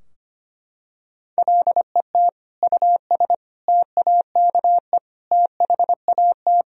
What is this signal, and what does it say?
Morse code, heard as a keyed tone, about 25 wpm, 700 Hz. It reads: LET US TAKE THAT
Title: Sentences from Top 1000 Words - Review Part 2 - 25wpm